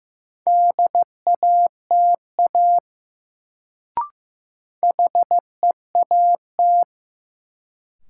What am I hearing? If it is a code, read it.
Morse code, heard as a keyed tone, about 15 wpm, 700 Hz. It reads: DATA HEAT
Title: Sentences from Top 1000 Words - Review Part 3 - New Word Drill - 15wpm